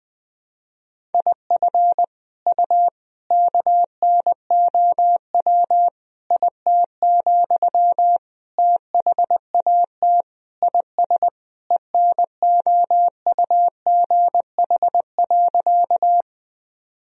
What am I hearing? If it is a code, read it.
Morse code, heard as a keyed tone, about 21 wpm, 700 Hz. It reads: IF U KNOW IT, THAT IS ENOUGH.
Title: Sentences from Top 400 Words and Abbreviations 20wpm